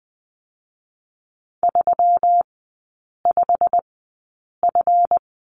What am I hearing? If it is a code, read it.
Morse code, heard as a keyed tone, about 20 wpm, spaced as for 8 wpm, 700 Hz. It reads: 35F
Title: 166 Lesson 27e - Number 7 - Three Characters 20wpm